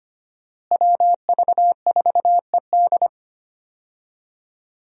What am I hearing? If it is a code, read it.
Morse code, heard as a keyed tone, about 25 wpm, 700 Hz. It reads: WV4ED
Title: USA Call Signs - 5 Characters - Repeated 2x - 25wpm